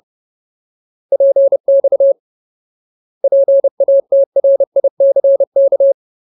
Morse code, heard as a keyed tone, about 30 wpm, 550 Hz. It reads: PX PATRICK
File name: Ultimate Word List / 30WPM Ultimate